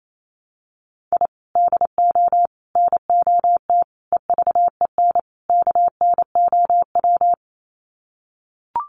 Morse code, heard as a keyed tone, about 28 wpm, 700 Hz. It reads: I DO NOT EVEN KNOW
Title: Sets of 5 Words 28wpm